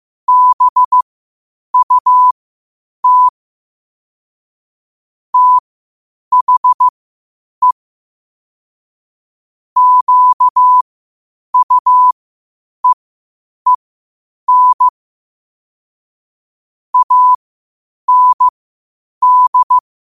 Morse code, heard as a keyed tone, about 15 wpm, spaced as for 8 wpm, 1000 Hz. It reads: BUT THE QUEEN AND
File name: 8wpm